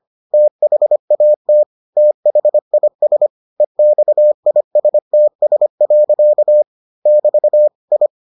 Morse code, heard as a keyed tone, about 25 wpm, 600 Hz. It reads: THAT THIS EXISTS. <BT> I